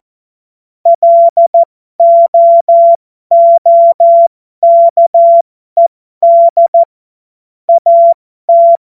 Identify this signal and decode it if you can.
Morse code, heard as a keyed tone, about 14 wpm, spaced as for 12 wpm, 680 Hz. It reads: LOOKED AT
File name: Morsekurs / Witze